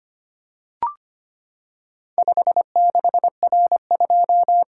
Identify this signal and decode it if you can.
Morse code, heard as a keyed tone, about 25 wpm, 700 Hz. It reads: 56R2
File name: practice / 25wpm / Four Letter-Number Combinations - 25wpm